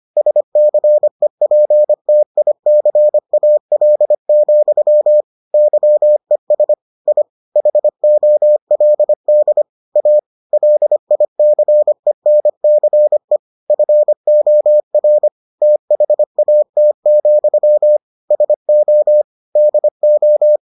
Morse code, heard as a keyed tone, about 25 wpm, 600 Hz. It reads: SCEPTICAL, YES I HOLD A LICENCE FOR THAT, SO DO